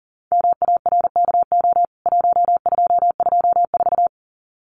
Morse code, heard as a keyed tone, about 37 wpm, 700 Hz. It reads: MARKO 1224
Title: CWT Exchange 40wpm